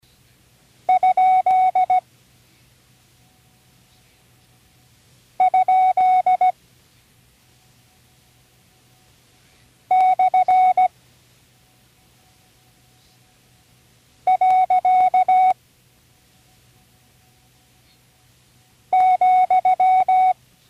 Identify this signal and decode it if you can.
Morse code, heard as a keyed tone, about 16 wpm, 740 Hz. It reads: ? ? / . ,
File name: CW Learning Programs / Learning Morse Code (from Tune in the World) / 40 punctuation review